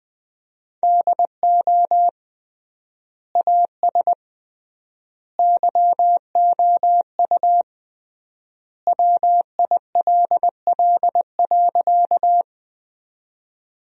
Morse code, heard as a keyed tone, about 20 wpm, 700 Hz. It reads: DO AS YOU WILL.
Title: 3x Word-Spacing - Sentences from Top 100 Words - 20wpm